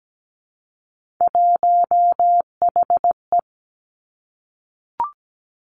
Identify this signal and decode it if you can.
Morse code, heard as a keyed tone, about 17 wpm, 700 Hz. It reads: 1HE
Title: Three Challenging Letter-Number Combinations 17wpm